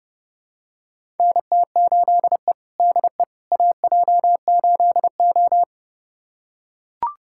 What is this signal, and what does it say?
Morse code, heard as a keyed tone, about 30 wpm, 700 Hz. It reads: NT8E DE AJ8O
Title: USA Callsign DE Callsign - 4 Characters - 30wpm